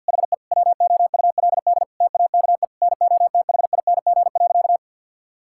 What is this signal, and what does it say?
Morse code, heard as a keyed tone, about 53 wpm, 700 Hz. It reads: HE WOULD TAKE NOTHING.
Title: Sentences from Top 1000 Words - Review Part 6 - 50wpm